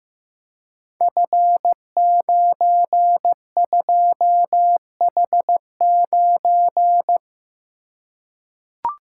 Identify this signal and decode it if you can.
Morse code, heard as a keyed tone, about 15 wpm, 700 Hz. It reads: F92H9
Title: Code Groups - 5 Letter:Numbers - 15wpm